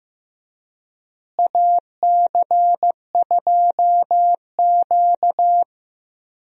Morse code, heard as a keyed tone, about 15 wpm, 700 Hz. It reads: AC2Q
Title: USA Call Signs - 4 Characters - Repeated 2x - 15wpm